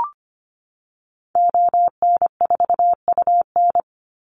Morse code, heard as a keyed tone, about 25 wpm, 700 Hz. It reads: ON4UN
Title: Call Signs - 25wpm